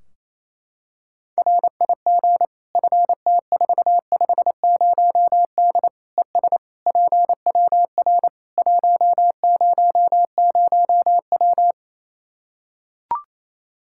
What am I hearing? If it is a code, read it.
Morse code, heard as a keyed tone, about 28 wpm, 700 Hz. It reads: RIG FT450D ES PWR 100W
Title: K1USN SST QSO Practice 28wpm